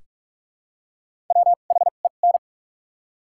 Morse code, heard as a keyed tone, about 42 wpm, spaced as for 31 wpm, 700 Hz. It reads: WHEN
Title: 2x Character Spacing - Top 100 Words - 45wpm